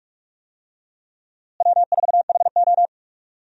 Morse code, heard as a keyed tone, about 44 wpm, 700 Hz. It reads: W4HO
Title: USA Call Signs - 4 Characters - Repeated 2x - 45wpm